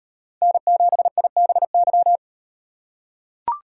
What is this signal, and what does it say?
Morse code, heard as a keyed tone, about 36 wpm, 700 Hz. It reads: N7IBY